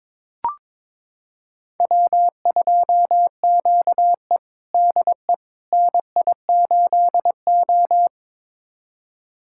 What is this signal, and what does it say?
Morse code, heard as a keyed tone, about 22 wpm, 700 Hz. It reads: W2QE DE NI8O
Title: USA Callsign DE Callsign - 4 Characters - 22wpm